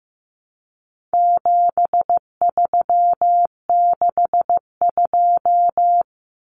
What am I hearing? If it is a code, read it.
Morse code, heard as a keyed tone, about 15 wpm, 700 Hz. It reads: 7362
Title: Numbers - 4-Digit 15wpm